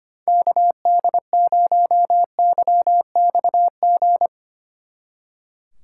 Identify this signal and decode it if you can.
Morse code, heard as a keyed tone, about 31 wpm, 700 Hz. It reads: KD0YXG
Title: Speed Racing - USA 2x3 Call Signs 25wpm